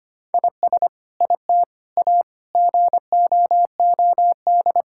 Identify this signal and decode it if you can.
Morse code, heard as a keyed tone, about 25 wpm, 700 Hz. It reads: IS IT A GOOD
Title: Sentences from Top 1000 Words - Review Part 5 - 25wpm